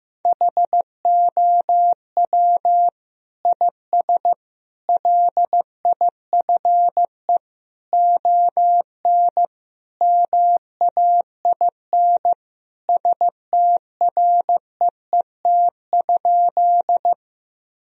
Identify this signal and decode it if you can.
Morse code, heard as a keyed tone, about 16 wpm, 700 Hz. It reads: HOW IS LIFE ON MAIN STREET?
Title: Sentences from Top 1000 Words - Review Part 2 - 15wpm